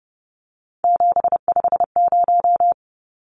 Morse code, heard as a keyed tone, about 30 wpm, 700 Hz. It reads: 750
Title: Numbers - 3-Digit 30wpm